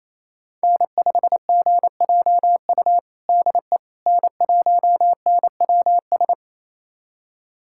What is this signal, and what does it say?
Morse code, heard as a keyed tone, about 28 wpm, 700 Hz. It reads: N5GJU DE N1NWS
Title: USA Callsign DE Callsign - 5 Characters - 28wpm